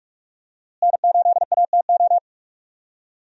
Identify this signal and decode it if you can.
Morse code, heard as a keyed tone, about 44 wpm, 700 Hz. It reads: N8ATO
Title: USA Call Signs - 5 Characters - Repeated 3x - 45wpm